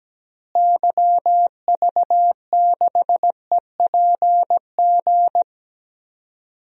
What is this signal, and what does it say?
Morse code, heard as a keyed tone, about 17 wpm, 700 Hz. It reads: YV6EPG
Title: DXCC-100 Call Signs - 6 Characters - Repeated 2x - 17wpm